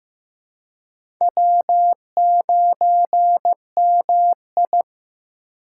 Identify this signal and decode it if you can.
Morse code, heard as a keyed tone, about 15 wpm, 700 Hz. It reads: W9MI
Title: USA Call Signs - 2x Character Spacing - 15wpm